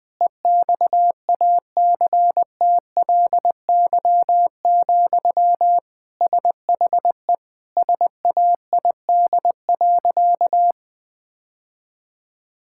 Morse code, heard as a keyed tone, about 20 wpm, 700 Hz. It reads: EXACTLY, SHE SAID.
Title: Sentences from Top 1000 Words - Review Part 6 - 20wpm